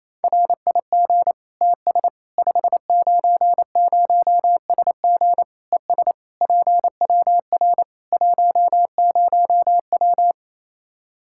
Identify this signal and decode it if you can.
Morse code, heard as a keyed tone, about 28 wpm, 700 Hz. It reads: RIG TS 590SG ES PWR 10W
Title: Practice QSO - Exchange 2 - Info - 28wpm